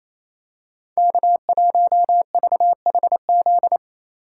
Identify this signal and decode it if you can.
Morse code, heard as a keyed tone, about 28 wpm, 700 Hz. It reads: K1VHZ
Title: USA Call Signs - 5 Characters - 6x Character Spacing - 28wpm